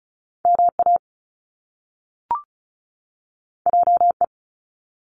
Morse code, heard as a keyed tone, about 36 wpm, 700 Hz. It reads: MA JE
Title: Top Two Letter Combinations in Words 35wpm